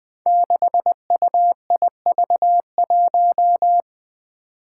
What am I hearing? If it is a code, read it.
Morse code, heard as a keyed tone, about 20 wpm, 700 Hz. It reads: 6UIV1
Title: Code Groups - 5 Letter:Numbers - 20wpm